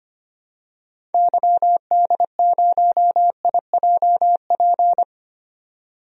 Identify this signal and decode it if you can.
Morse code, heard as a keyed tone, about 25 wpm, 700 Hz. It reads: YD0IJP
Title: DXCC-100 Call Signs - 6 Characters - 25wpm